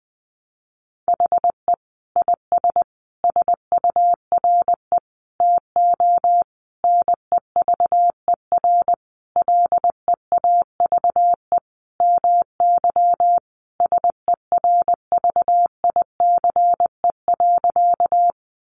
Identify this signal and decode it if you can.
Morse code, heard as a keyed tone, about 20 wpm, 700 Hz. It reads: HE IS SURE TO NEVER LEAVE MY SERVICE.
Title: Sentences from Top 700 Words 20wpm